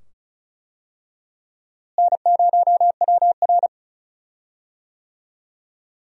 Morse code, heard as a keyed tone, about 34 wpm, 700 Hz. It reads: N0WR
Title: USA Call Signs - 4 Characters - Repeated 2x - 35wpm